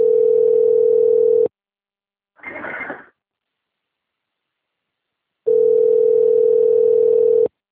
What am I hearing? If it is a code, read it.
Ringback tone (North America), 440 + 480 Hz.